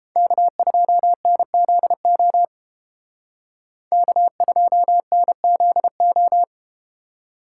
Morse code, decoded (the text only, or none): K2NZO K2NZO